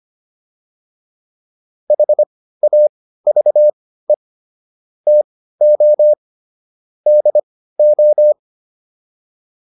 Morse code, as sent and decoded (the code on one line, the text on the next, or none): .... .- ...- . / - --- / -.. ---
HAVE TO DO